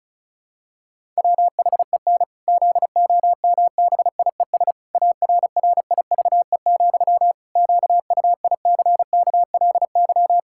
Morse code, heard as a keyed tone, about 36 wpm, 700 Hz. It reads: WHEN ZOMBIES ARRIVE, QUICKLY
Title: Pangram Challenge v2 - 35wpm